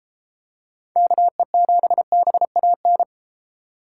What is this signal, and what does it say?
Morse code, heard as a keyed tone, about 32 wpm, 700 Hz. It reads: KE7BAN